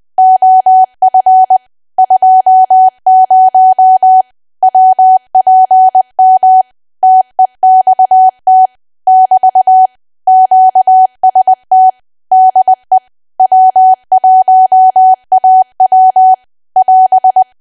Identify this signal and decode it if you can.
Morse code, heard as a keyed tone, about 20 wpm, 750 Hz. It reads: OF 20 WPM TEXT <BT> QST DE W1AW <AS>
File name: cwa / W1AW-Level3 / 20WPM